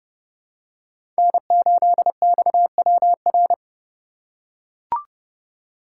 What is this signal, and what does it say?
Morse code, heard as a keyed tone, about 30 wpm, 700 Hz. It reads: N8XWR
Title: USA Call Signs - 4x Character Spacing - 30wpm